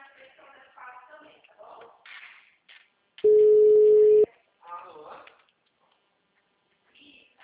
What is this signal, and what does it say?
Ringback tone (Europe, 425 Hz), with speech.